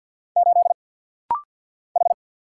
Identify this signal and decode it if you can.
Morse code, heard as a keyed tone, about 49 wpm, 700 Hz. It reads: 8 H
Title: Single Letter-Number - Rapid-Fire 50wpm